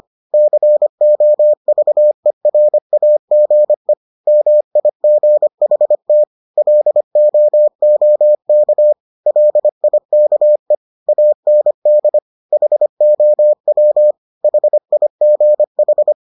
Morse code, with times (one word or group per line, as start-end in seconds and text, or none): 0.3-3.9 COVERAGE
4.3-6.2 MIGHT
6.6-8.9 LOOK
9.3-10.8 LIKE
11.1-12.2 AND
12.5-14.1 HOW
14.4-16.1 HIGH